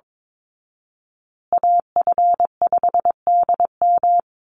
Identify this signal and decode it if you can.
Morse code, heard as a keyed tone, about 22 wpm, 700 Hz. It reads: AF5DM
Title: Call Signs Repeated 2x - 22wpm